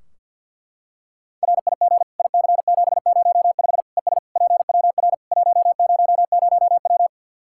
Morse code, heard as a keyed tone, about 50 wpm, 700 Hz. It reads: RIG IC705 ES PWR 100W
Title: Practice QSO - Exchange 2 - Info - 50wpm